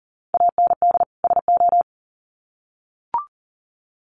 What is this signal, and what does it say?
Morse code, heard as a keyed tone, about 40 wpm, 700 Hz. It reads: AND SO
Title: Sets of 2 Words 40wpm